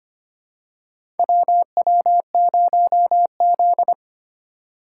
Morse code, heard as a keyed tone, about 25 wpm, 700 Hz. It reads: WW0Z
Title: USA Call Signs - 4 Characters - 6x Character Spacing - 25wpm